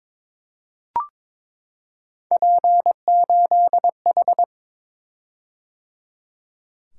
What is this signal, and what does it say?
Morse code, heard as a keyed tone, about 22 wpm, 700 Hz. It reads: P8H
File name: practice / 22wpm / Three Letter-Number Combinations - 22wpm